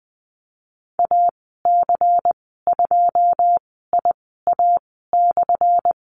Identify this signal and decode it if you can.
Morse code, heard as a keyed tone, about 19 wpm, 700 Hz. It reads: A C 2 I A /
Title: 179 Lesson 29g - Slash - Call Signs 20wpm